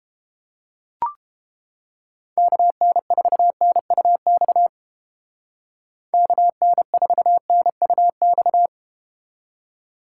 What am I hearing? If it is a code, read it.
Morse code, heard as a keyed tone, about 30 wpm, 700 Hz. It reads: KN4NUX KN4NUX